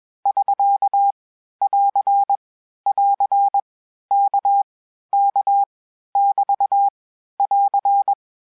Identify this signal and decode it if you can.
Morse code, heard as a keyed tone, about 21 wpm, 800 Hz. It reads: <SK> <AR> <AR> K K <BT> <AR>
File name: cours-du-lundi files / cours-1580-abreviations-mpm-20